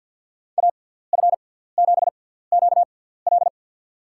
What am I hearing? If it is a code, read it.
Morse code, heard as a keyed tone, about 48 wpm, spaced as for 17 wpm, 700 Hz. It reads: AF7QL
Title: USA Call Signs - 5 Characters - 6x Character Spacing - 50wpm